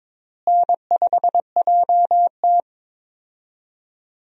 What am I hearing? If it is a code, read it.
Morse code, heard as a keyed tone, about 21 wpm, 700 Hz. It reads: N5JT
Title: USA Call Signs - 4 Characters - Repeated 3x - 22wpm